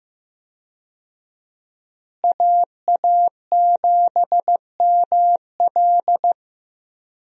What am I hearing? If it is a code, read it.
Morse code, heard as a keyed tone, about 15 wpm, 700 Hz. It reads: AA7ML
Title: USA Call Signs - 5 Characters - Repeated 3x - 15wpm